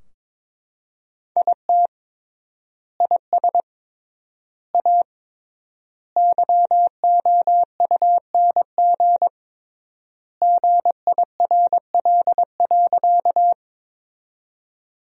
Morse code, heard as a keyed tone, about 21 wpm, 700 Hz. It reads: IT IS A YOUNG GIRL.
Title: 3x Word-Spacing - Sentences from Top 400 Words - 22wpm